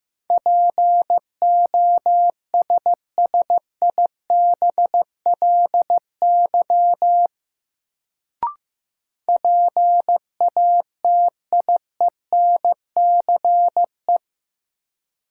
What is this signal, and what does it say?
Morse code, heard as a keyed tone, about 15 wpm, 700 Hz. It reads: POSSIBLY PATIENCE
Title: Sentences from Top 1300 Words - New Word Drill - 15wpm